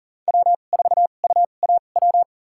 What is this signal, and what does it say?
Morse code, heard as a keyed tone, about 39 wpm, spaced as for 29 wpm, 700 Hz. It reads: W4UAW
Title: USA Call Signs - 5 Characters - 2x Character Spacing - 40wpm